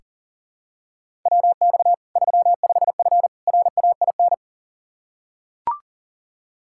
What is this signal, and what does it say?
Morse code, heard as a keyed tone, about 40 wpm, 700 Hz. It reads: WX 35F RAIN
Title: K1USN SST QSO Practice 40wpm